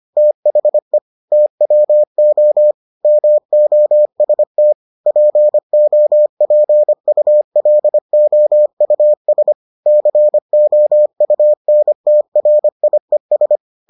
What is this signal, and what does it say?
Morse code, heard as a keyed tone, about 25 wpm, 600 Hz. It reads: THE TWO MOST POPULOUS COUNTRIES